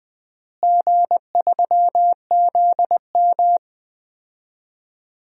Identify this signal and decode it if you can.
Morse code, heard as a keyed tone, about 20 wpm, 700 Hz. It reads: G3ZM
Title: DXCC-100 Call Signs - 4 Characters - Repeated 3x - 20wpm